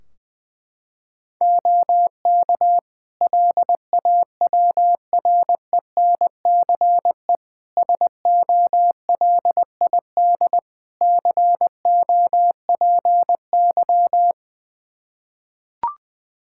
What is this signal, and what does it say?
Morse code, heard as a keyed tone, about 20 wpm, 700 Hz. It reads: OK LAWRENCE SOLID COPY
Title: Practice QSO - Exchange 3 - More Info - 20wpm